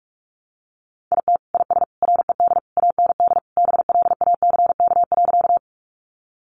Morse code, heard as a keyed tone, about 44 wpm, 700 Hz. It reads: IT IS RED AND BLACK.
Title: Easy Phrases 45wpm